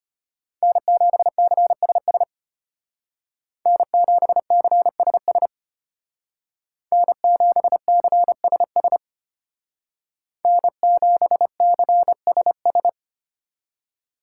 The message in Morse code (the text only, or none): N7CSS N7CSS N7CSS N7CSS